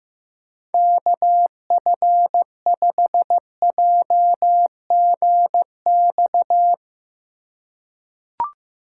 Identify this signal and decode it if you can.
Morse code, heard as a keyed tone, about 15 wpm, 700 Hz. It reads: KF5JGX
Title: USA Call Signs - 6 Characters - 2x Character Spacing - 15wpm